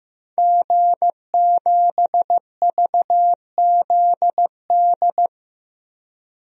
Morse code, heard as a keyed tone, about 15 wpm, 700 Hz. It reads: G7VZD
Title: DXCC-100 Call Signs - 5 Characters - Repeated 3x - 15wpm